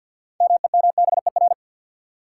Morse code, heard as a keyed tone, about 53 wpm, 700 Hz. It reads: MEMBER
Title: Sets of 3 Words - New Word Drill - 50wpm